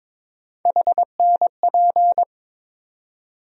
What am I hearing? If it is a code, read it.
Morse code, heard as a keyed tone, about 22 wpm, 700 Hz. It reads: HNP
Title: USA Call Sign Postfixes - 3 Letters 22wpm